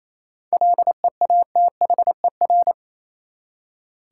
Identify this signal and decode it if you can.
Morse code, heard as a keyed tone, about 28 wpm, 700 Hz. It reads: LEATHER
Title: Seven Letter Words - 28wpm